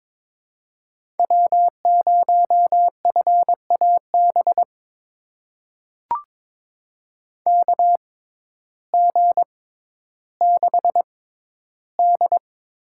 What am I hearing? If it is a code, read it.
Morse code, heard as a keyed tone, about 22 wpm, spaced as for 12 wpm, 700 Hz. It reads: W0FAB KG6D
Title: USA Call Signs - 6x Character Spacing - 22wpm